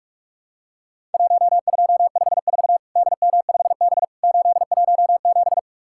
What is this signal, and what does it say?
Morse code, heard as a keyed tone, about 46 wpm, 700 Hz. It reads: 1254 DM5B 817
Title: Worked All Europe DX Contest - QTC Exchange - 45wpm